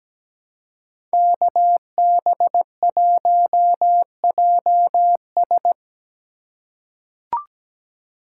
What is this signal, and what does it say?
Morse code, heard as a keyed tone, about 17 wpm, 700 Hz. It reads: KB1JS
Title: USA Call Signs - 5 Characters - 2x Character Spacing - 17wpm